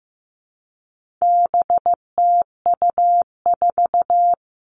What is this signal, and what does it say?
Morse code, heard as a keyed tone, about 15 wpm, 700 Hz. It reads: BTU4
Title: All Cut Numbers - 4-Digit 15wpm